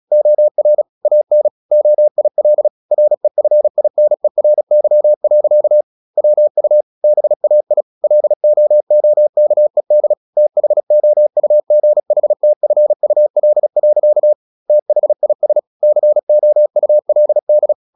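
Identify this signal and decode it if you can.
Morse code, heard as a keyed tone, about 36 wpm, 600 Hz. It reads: OR AN OIL REFINERY. WU BAI LOOKED THOUGHTFUL. THIS COULD
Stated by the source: Morse practice at 36 wpm, text of Ken Follett's novel Never, from CW med Gnister.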